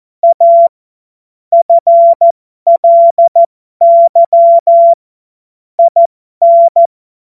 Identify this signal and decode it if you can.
Morse code, heard as a keyed tone, about 14 wpm, spaced as for 12 wpm, 680 Hz. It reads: A FLY IN
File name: Morsekurs / Witze